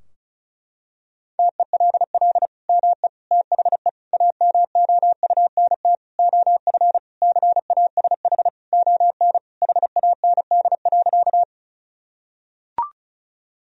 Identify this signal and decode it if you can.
Morse code, heard as a keyed tone, about 36 wpm, 700 Hz. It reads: TELL ME THE AMOUNT OF CASH ON HAND.
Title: Sentences from Top 1000 Words - Review Part 4 - 35wpm